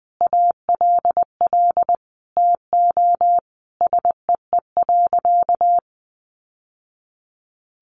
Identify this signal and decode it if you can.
Morse code, heard as a keyed tone, about 20 wpm, 700 Hz. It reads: ALL TO SEE.
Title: Sentences from Top 1200 Words 20wpm